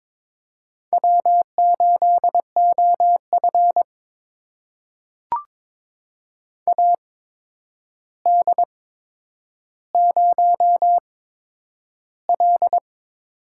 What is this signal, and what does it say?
Morse code, heard as a keyed tone, about 22 wpm, spaced as for 9 wpm, 700 Hz. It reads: W8OF AD0L
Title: USA Call Signs - 8x Character Spacing - 22wpm